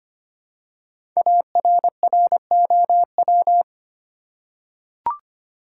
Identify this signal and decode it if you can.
Morse code, heard as a keyed tone, about 25 wpm, 700 Hz. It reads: ARROW
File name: 5 Letter Words - First 3 Letters Given - 25wpm